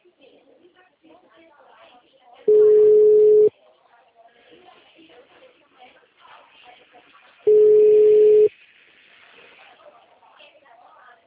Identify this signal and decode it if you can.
Ringback tone (Europe, 425 Hz), with speech.
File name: out- -2030- - - .34193